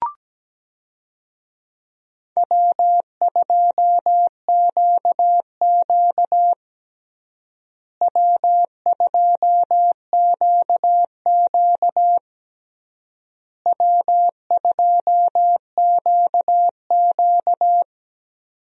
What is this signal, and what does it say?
Morse code, heard as a keyed tone, about 17 wpm, 700 Hz. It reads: W2QQ W2QQ W2QQ
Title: USA Call Signs - 4 Characters - Repeated 3x - 17wpm